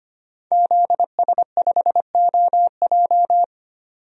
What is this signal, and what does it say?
Morse code, heard as a keyed tone, about 25 wpm, 700 Hz. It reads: ZS5OJ
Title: DXCC-100 Call Signs - 5 Characters - 25wpm